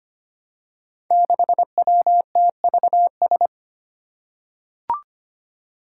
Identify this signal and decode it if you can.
Morse code, heard as a keyed tone, about 25 wpm, 700 Hz. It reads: 6WTVS
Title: Five Challenging Letter-Number Combinations 25wpm